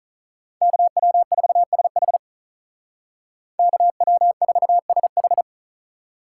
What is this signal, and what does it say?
Morse code, heard as a keyed tone, about 44 wpm, 700 Hz. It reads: KW4SH KW4SH